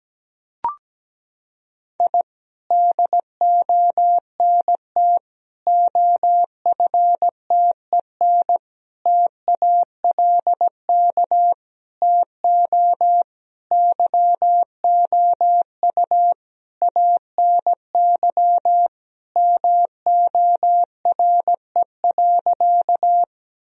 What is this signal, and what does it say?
Morse code, heard as a keyed tone, about 17 wpm, 700 Hz. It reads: I DONT OFTEN TALK TO YOU ANY MORE.
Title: Sentences from Top 1000 Words - Review Part 1 - 17wpm